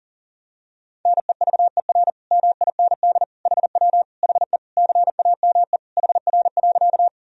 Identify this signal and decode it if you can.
Morse code, heard as a keyed tone, about 40 wpm, 700 Hz. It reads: NEVER MIND HW HE CAME HR.
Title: Sentences from Top 500 Words and Abbreviations 40wpm